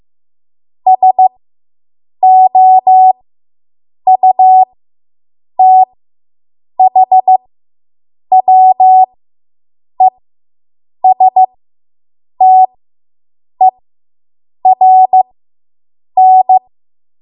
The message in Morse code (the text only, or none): SOUTHWESTERN